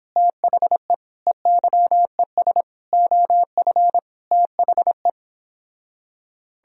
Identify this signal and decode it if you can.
Morse code, heard as a keyed tone, about 31 wpm, 700 Hz. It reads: THE EYES OF THE